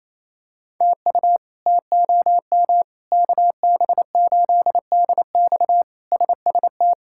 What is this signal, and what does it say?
Morse code, heard as a keyed tone, about 28 wpm, 700 Hz. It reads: TU TOM KB8DX SST
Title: K1USN SST QSO Practice - 28wpm